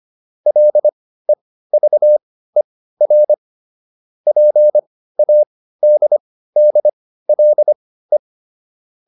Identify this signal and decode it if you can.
Morse code, heard as a keyed tone, about 25 wpm, spaced as for 15 wpm, 600 Hz. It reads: LEVER PADDLE